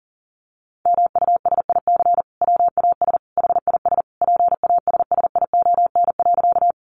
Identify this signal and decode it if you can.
Morse code, heard as a keyed tone, about 40 wpm, 700 Hz. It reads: MUSIC WAS HIS PASSION.
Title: Sentences from Top 1300 Words 40wpm